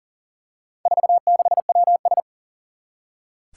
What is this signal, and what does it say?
Morse code, heard as a keyed tone, about 40 wpm, 700 Hz. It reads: 46WS
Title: Four Challenging Letter-Number Combinations 40wpm